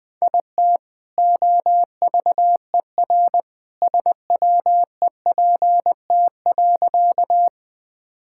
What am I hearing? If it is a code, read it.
Morse code, heard as a keyed tone, about 20 wpm, 700 Hz. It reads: IT OVER SWEPT.
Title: All-Age Appropriate Jokes - 20wpm